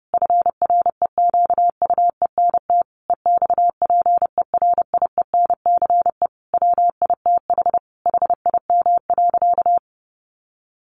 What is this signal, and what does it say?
Morse code, heard as a keyed tone, about 30 wpm, 700 Hz. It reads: FREQUENT EXPERIENCE WITH HIM.
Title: Sentences from Top 1400 Words 30wpm